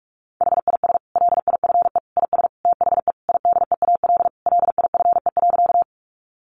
Morse code, heard as a keyed tone, about 44 wpm, 700 Hz. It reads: HIS LIFE IS THE IDEAL LIFE.
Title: Sentences from Top 1400 Words 45wpm